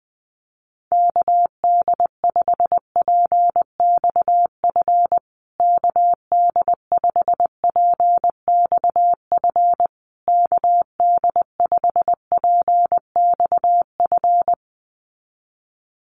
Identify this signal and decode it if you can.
Morse code, heard as a keyed tone, about 20 wpm, 700 Hz. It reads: KD5PXF KD5PXF KD5PXF
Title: USA Call Signs Repeated 3x 20wpm